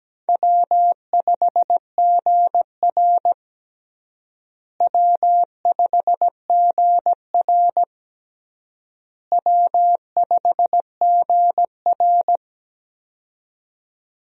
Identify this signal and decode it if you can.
Morse code, heard as a keyed tone, about 17 wpm, 700 Hz. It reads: W5GR W5GR W5GR
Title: USA Call Signs - 4 Characters - Repeated 3x - 17wpm